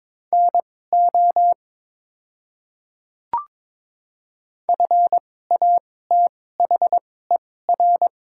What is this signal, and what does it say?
Morse code, heard as a keyed tone, about 22 wpm, spaced as for 16 wpm, 690 Hz. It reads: NO FATHER
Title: 2x Character Spacing - Top 200 Words - 22wpm